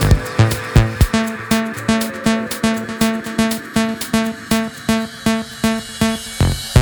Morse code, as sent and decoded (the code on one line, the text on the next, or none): none